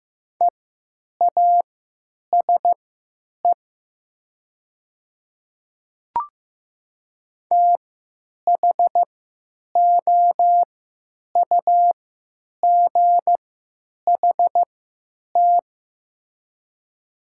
Morse code, heard as a keyed tone, about 15 wpm, spaced as for 9 wpm, 700 Hz. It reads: EASE THOUGHT
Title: 3x Character Spacing - Top 300 Words - 15wpm